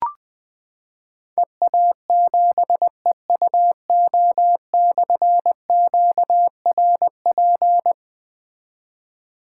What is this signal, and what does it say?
Morse code, heard as a keyed tone, about 20 wpm, 700 Hz. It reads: EA7EUO/QRP
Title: Difficult Call Signs 20wpm